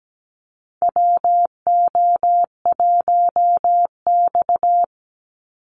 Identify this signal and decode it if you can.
Morse code, heard as a keyed tone, about 17 wpm, 700 Hz. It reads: WO1X
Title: Call Signs Repeated 2x - 17wpm